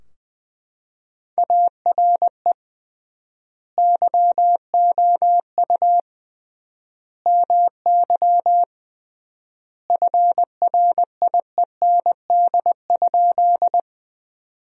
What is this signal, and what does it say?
Morse code, heard as a keyed tone, about 20 wpm, 700 Hz. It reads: ARE YOU MY FRIEND?
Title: 3x Word-Spacing - Sentences from Top 300 Words - 20wpm